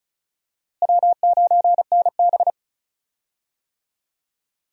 Morse code, heard as a keyed tone, about 35 wpm, 700 Hz. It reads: W9NB
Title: USA Call Signs - 4 Characters - Repeated 2x - 35wpm